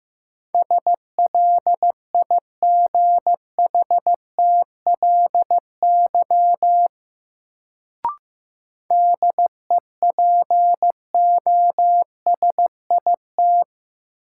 Morse code, heard as a keyed tone, about 15 wpm, 700 Hz. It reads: SLIGHTLY DEPOSIT
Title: Sentences from Top 1300 Words - New Word Drill Alt - 15wpm